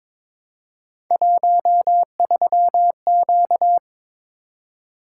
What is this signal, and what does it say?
Morse code, heard as a keyed tone, about 22 wpm, 700 Hz. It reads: 13Q
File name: practice / 22wpm / Three Letter-Number Combinations - 22wpm